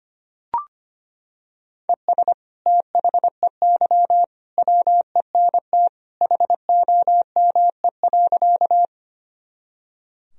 Morse code, heard as a keyed tone, about 25 wpm, 700 Hz. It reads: ES THEY WENT HOME.
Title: Sentences from Top 200 Words and Abbreviations 25wpm